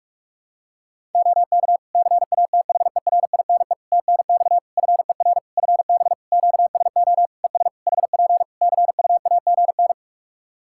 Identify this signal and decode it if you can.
Morse code, heard as a keyed tone, about 46 wpm, 700 Hz. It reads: OK CATHERINE TNX FER FB QSO ES HP CUAGN
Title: Practice QSO - Exchange 4 - Ending - 45wpm